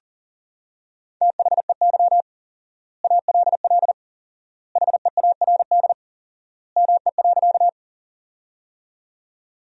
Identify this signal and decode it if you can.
Morse code, heard as a keyed tone, about 37 wpm, 700 Hz. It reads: THEY ALL HEARD ME.
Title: 4x Word-Spacing - Sentences from Top 400 Words - 40wpm